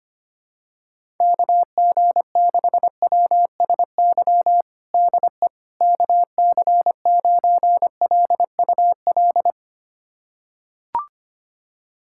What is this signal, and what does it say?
Morse code, heard as a keyed tone, about 25 wpm, 700 Hz. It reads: KG6WSY DE KC9LUL
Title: USA Callsign DE Callsign - 6 Characters - 25wpm